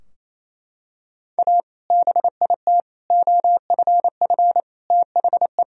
Morse code, heard as a keyed tone, about 28 wpm, 700 Hz. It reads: A BIT OFF THE
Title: Sentences from Top 1000 Words - Review Part 2 - 28wpm